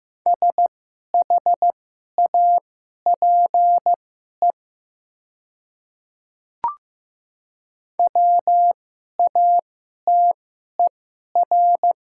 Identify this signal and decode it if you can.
Morse code, heard as a keyed tone, about 16 wpm, spaced as for 11 wpm, 700 Hz. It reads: SHAPE WATER
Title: 2x Character Spacing - Top 500 Words - 15wpm